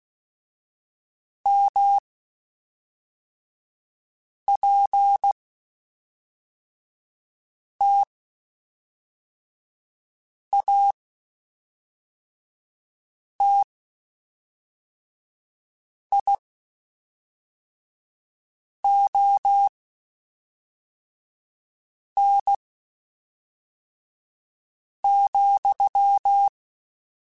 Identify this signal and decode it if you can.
Morse code, heard as a keyed tone, about 16 wpm, spaced as for 3 wpm, 800 Hz. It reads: MPTATION,
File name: Spaced / Chuckles Spaced-mp3s / Temptation sp @5